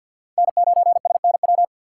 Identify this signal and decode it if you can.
Morse code, heard as a keyed tone, about 49 wpm, 700 Hz. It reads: N9SNW
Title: USA Call Signs - 5 Characters - 3x Character Spacing - 50wpm